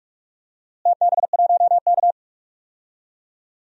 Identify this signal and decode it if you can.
Morse code, heard as a keyed tone, about 45 wpm, 700 Hz. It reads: TB1K
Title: DXCC-100 Call Signs - 4 Characters - Repeated 2x - 45wpm